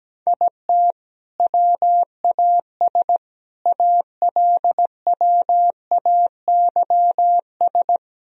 Morse code, heard as a keyed tone, about 17 wpm, 700 Hz. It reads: IT WAS ALWAYS
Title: Sentences from Top 1000 Words - Review Part 4 - 17wpm